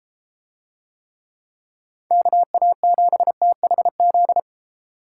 Morse code, heard as a keyed tone, about 32 wpm, 700 Hz. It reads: KA7THZ